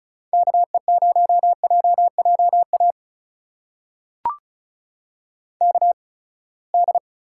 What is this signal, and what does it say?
Morse code, heard as a keyed tone, about 36 wpm, spaced as for 24 wpm, 700 Hz. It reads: KE0JJA KD
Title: USA Call Signs - 8x Character Spacing - 35wpm